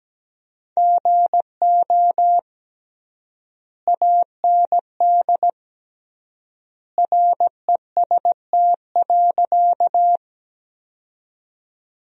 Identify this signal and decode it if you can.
Morse code, heard as a keyed tone, about 17 wpm, 700 Hz. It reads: GO AND REST.
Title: 3x Word-Spacing - Sentences from Top 500 Words - 17wpm